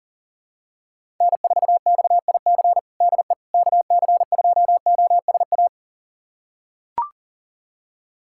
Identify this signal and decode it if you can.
Morse code, heard as a keyed tone, about 40 wpm, 700 Hz. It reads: N4XIC DE KC2OSA
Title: USA Callsign DE Callsign 40wpm